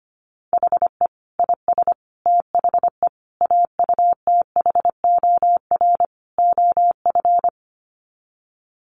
Morse code, heard as a keyed tone, about 25 wpm, 700 Hz. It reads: HE IS THE AUTHOR OF
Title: Sets of 5 Words Encore2 25wpm